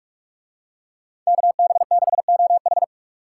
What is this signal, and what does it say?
Morse code, heard as a keyed tone, about 46 wpm, 700 Hz. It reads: KB6OH